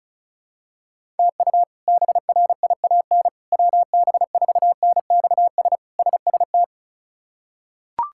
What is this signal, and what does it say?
Morse code, heard as a keyed tone, about 36 wpm, 700 Hz. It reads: TU BRIAN WB4NXS SST
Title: K1USN SST QSO Practice - 35wpm